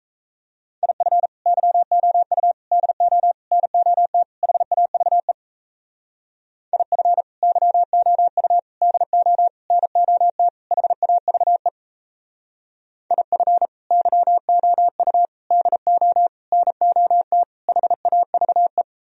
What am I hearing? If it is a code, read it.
Morse code, heard as a keyed tone, about 36 wpm, 700 Hz. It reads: IF YOU DO NOT HAVE IF YOU DO NOT HAVE IF YOU DO NOT HAVE